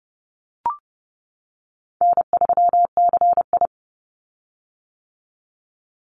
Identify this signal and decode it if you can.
Morse code, heard as a keyed tone, about 30 wpm, 700 Hz. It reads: N3CI
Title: Call Signs - 30wpm